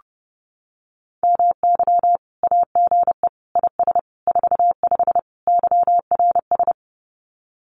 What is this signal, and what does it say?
Morse code, heard as a keyed tone, about 32 wpm, 700 Hz. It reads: MY AGE IS 45 YRS
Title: Common QSO Phrases Encore 30wpm